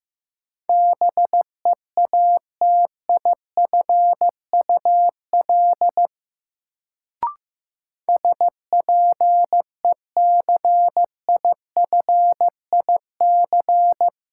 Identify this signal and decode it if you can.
Morse code, heard as a keyed tone, about 15 wpm, 700 Hz. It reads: BEATIFUL SPECIFIC
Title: Sentences from Top 1300 Words - New Word Drill Alt - 15wpm